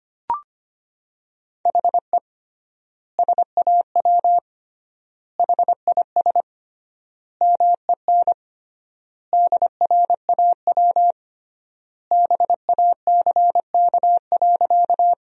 Morse code, heard as a keyed tone, about 25 wpm, 700 Hz. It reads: HE SAW HIS MEN DRAW BACK.
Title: 3x Word-Spacing - Sentences from Top 300 Words - 25wpm